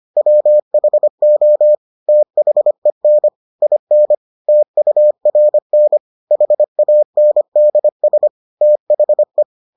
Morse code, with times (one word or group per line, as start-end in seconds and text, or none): 0.2-1.8 WHO
2.1-3.3 THEN
3.6-4.1 IN
4.5-6.0 TURN
6.3-8.3 HANDS
8.6-9.4 THE